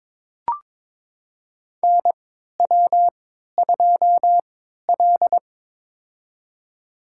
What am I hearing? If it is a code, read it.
Morse code, heard as a keyed tone, about 22 wpm, spaced as for 12 wpm, 700 Hz. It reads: NW2L
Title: USA Call Signs - 3x Character Spacing - 22wpm